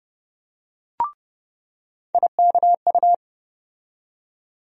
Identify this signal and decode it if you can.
Morse code, heard as a keyed tone, about 30 wpm, 700 Hz. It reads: IKU
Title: USA Call Sign Postfixes - 3 Letters 30wpm